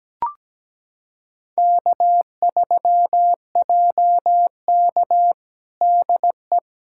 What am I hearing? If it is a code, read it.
Morse code, heard as a keyed tone, about 17 wpm, 700 Hz. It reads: K3JK DE
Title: USA Callsign DE Callsign - 4 Characters - 17wpm